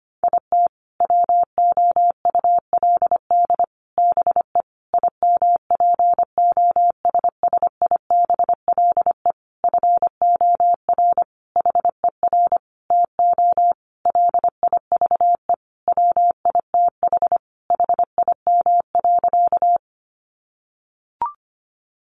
Morse code, heard as a keyed tone, about 25 wpm, 700 Hz. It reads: IT WOULD BE IMPOSSIBLE FOR HER TO LIVE WITH HIM.
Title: Sentences from Top 1100 Words 25wpm